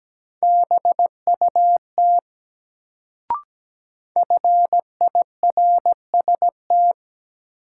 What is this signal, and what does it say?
Morse code, heard as a keyed tone, about 17 wpm, 700 Hz. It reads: BUT FIRST
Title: Sets of 2 Words - New Word Drill - 17wpm